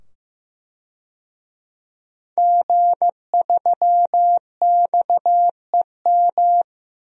Morse code, heard as a keyed tone, about 15 wpm, 700 Hz. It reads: G3XEM
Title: DXCC-100 Call Signs - 5 Characters - 15wpm